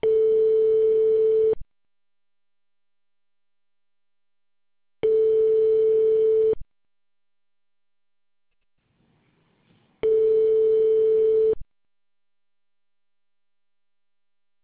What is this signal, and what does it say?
Ringback tone (France), 440 Hz.